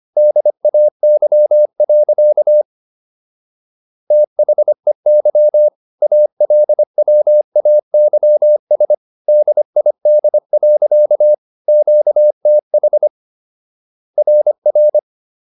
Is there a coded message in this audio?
Yes: Morse code from 0.2 to 15.0 s, heard as a keyed tone, about 25 wpm, 600 Hz. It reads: DAY. THEY ALWAYS DID. QTH RR